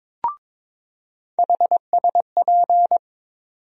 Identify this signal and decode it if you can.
Morse code, heard as a keyed tone, about 22 wpm, 700 Hz. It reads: HSP
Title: USA Call Sign Postfixes - 3 Letters 22wpm